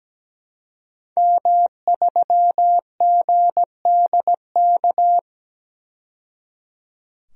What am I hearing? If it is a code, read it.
Morse code, heard as a keyed tone, about 17 wpm, 700 Hz. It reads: M3GDK
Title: DXCC-100 Call Signs - 5 Characters - 17wpm